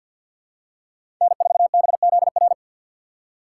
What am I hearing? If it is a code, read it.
Morse code, heard as a keyed tone, about 51 wpm, 700 Hz. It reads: N4BZR
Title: USA Call Signs - 3x Character Spacing - 50wpm